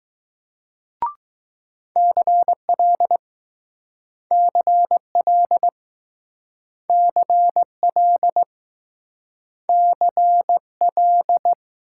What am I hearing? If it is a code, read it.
Morse code, heard as a keyed tone, about 20 wpm, 700 Hz. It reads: CL CL CL CL